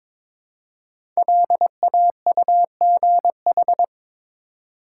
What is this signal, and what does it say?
Morse code, heard as a keyed tone, about 22 wpm, 700 Hz. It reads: LAUGH
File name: 5 Letter Words - First 3 Letters Given - 22wpm